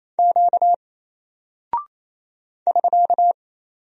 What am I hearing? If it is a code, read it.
Morse code, heard as a keyed tone, about 27 wpm, 700 Hz. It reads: Q <SK>
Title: Single Character (Letter:Number:Punctuation:Prosign) 28wpm